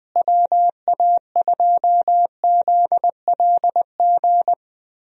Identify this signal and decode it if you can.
Morse code, heard as a keyed tone, about 20 wpm, 700 Hz. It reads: WA2ZLG
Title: USA Call Signs - 6 Characters - Repeated 3x - 20wpm